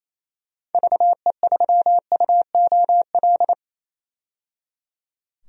Morse code, heard as a keyed tone, about 27 wpm, 700 Hz. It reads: VE3UOL
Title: DXCC-100 Call Signs - 6 Characters - 28wpm